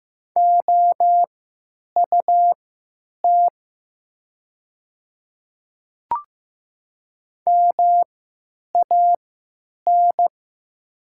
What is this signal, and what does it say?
Morse code, heard as a keyed tone, about 15 wpm, spaced as for 9 wpm, 700 Hz. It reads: OUT MAN
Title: 3x Character Spacing - Top 200 Words - 15wpm